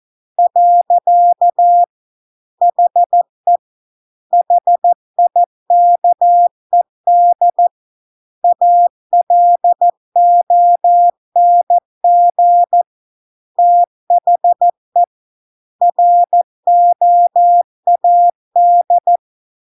Morse code, heard as a keyed tone, about 14 wpm, 700 Hz. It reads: . HE HIKED ALONG THE ROAD